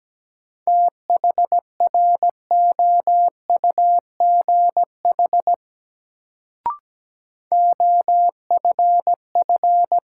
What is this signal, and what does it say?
Morse code, heard as a keyed tone, about 17 wpm, 700 Hz. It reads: THROUGH OFF
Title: Sentences from Top 200 Words - New Word Drill - 17wpm